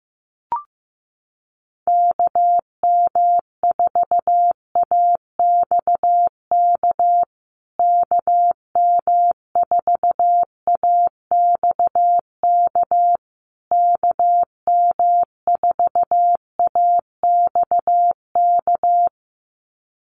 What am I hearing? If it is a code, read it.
Morse code, heard as a keyed tone, about 15 wpm, 700 Hz. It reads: KM4AXK KM4AXK KM4AXK
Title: USA Call Signs Repeated 3x 15wpm